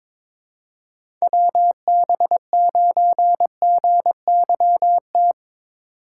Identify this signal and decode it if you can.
Morse code, heard as a keyed tone, about 22 wpm, 700 Hz. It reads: WB9GYT
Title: USA Call Signs - 6 Characters - 3x Character Spacing - 22wpm